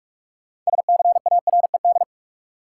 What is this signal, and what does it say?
Morse code, heard as a keyed tone, about 46 wpm, 700 Hz. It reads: SCARED
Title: Sentences from Top 700 Words - New Word Drill - 45wpm